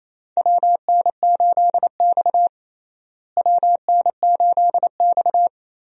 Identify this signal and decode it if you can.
Morse code, heard as a keyed tone, about 28 wpm, 700 Hz. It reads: WN8X WN8X
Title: USA Call Signs - 4 Characters - Repeated 3x - 28wpm